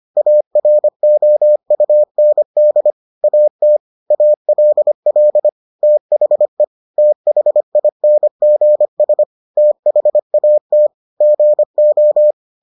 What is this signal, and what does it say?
Morse code, heard as a keyed tone, about 25 wpm, 600 Hz. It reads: AROUND AT ALL THE THINGS THAT GO